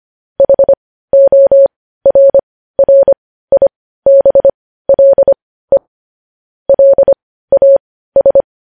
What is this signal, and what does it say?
Morse code, heard as a keyed tone, about 25 wpm, spaced as for 15 wpm, 550 Hz. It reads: HORRIBLE LAS
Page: news headlines in morse code at 15 wpm